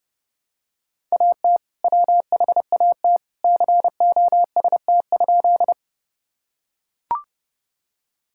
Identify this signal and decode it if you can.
Morse code, heard as a keyed tone, about 30 wpm, 700 Hz. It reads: AT WHAT COST?
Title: Sentences from Top 1000 Words - Review Part 5 - 30wpm